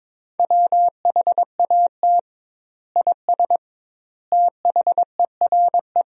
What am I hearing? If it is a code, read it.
Morse code, heard as a keyed tone, about 22 wpm, 700 Hz. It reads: WHAT IS THERE
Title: 2x Word-Spacing - Sentences from Top 100 Words - 22wpm